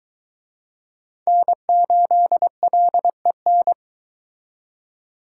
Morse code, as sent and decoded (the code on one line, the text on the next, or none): -. ---.. .-.. -.
N8LEN